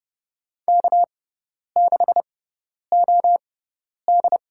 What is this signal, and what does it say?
Morse code, heard as a keyed tone, about 30 wpm, spaced as for 10 wpm, 700 Hz. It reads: K6OD
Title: USA Call Signs - 4 Characters - 6x Character Spacing - 30wpm